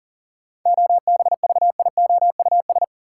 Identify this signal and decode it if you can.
Morse code, heard as a keyed tone, about 40 wpm, 700 Hz. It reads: OBVIOUS
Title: Sentences from Top 1600 Words - New Word Drill - 40wpm